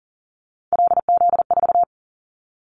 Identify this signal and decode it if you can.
Morse code, heard as a keyed tone, about 40 wpm, 700 Hz. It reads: LZ4
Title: Callsign Prefixes for Top 25 DXCC - 40wpm